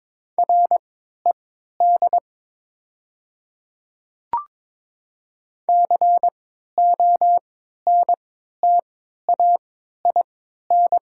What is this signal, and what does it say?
Morse code, heard as a keyed tone, about 22 wpm, spaced as for 13 wpm, 700 Hz. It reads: RED CONTAIN
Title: 3x Character Spacing - Top 500 Words - 22wpm